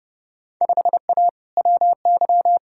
Morse code, heard as a keyed tone, about 32 wpm, 700 Hz. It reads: 5A WY
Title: Field Day Exchanges Encore - 30wpm